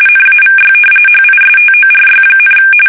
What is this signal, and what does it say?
RTTY, about 45.45 baud. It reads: NAVY HISTORY,
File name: fsk850 / battleshipnj